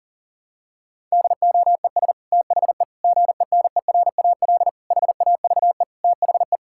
Morse code, heard as a keyed tone, about 40 wpm, 700 Hz. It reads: DOES THE GENERAL HAVE THE
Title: Sentences from Top 1000 Words - Review Part 6 - 40wpm